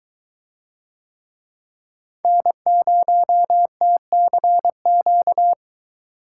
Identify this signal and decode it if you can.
Morse code, heard as a keyed tone, about 23 wpm, 700 Hz. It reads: N0TCQ